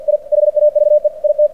Morse code, heard as a keyed tone, about 48 wpm, 600 Hz. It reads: I HAVE IT
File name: archived / CRSnet